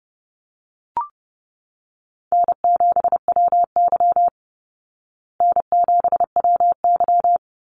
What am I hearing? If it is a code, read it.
Morse code, heard as a keyed tone, about 30 wpm, 700 Hz. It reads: N7WY N7WY
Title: Easy Call Signs Repeated 2x - 30wpm